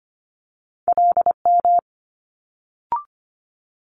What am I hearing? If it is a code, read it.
Morse code, heard as a keyed tone, about 25 wpm, 700 Hz. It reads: LM
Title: Multi-letter Phonemes Quick Fire 25wpm